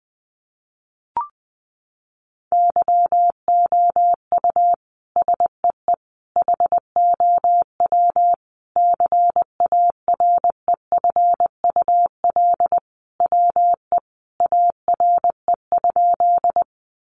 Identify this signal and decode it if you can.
Morse code, heard as a keyed tone, about 20 wpm, 700 Hz. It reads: YOU SEE HOW CAREFUL WE ARE?
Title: A Sound Of Thunder 20wpm